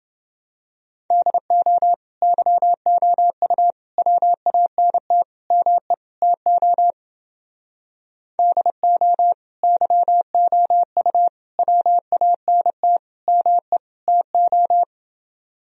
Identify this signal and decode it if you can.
Morse code, heard as a keyed tone, about 29 wpm, 700 Hz. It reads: DO YOU WANT ME TO DO YOU WANT ME TO